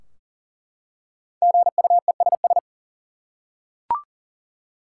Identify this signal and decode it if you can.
Morse code, heard as a keyed tone, about 40 wpm, 700 Hz. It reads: GUESS
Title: Five Letter Words - 40wpm